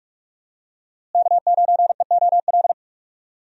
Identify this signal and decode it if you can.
Morse code, heard as a keyed tone, about 44 wpm, 700 Hz. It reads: K9EOL
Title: USA Call Signs - 5 Characters - Repeated 2x - 45wpm